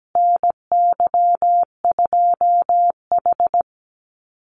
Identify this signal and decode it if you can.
Morse code, heard as a keyed tone, about 17 wpm, 700 Hz. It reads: NY2H
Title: Call Signs Repeated 2x - 17wpm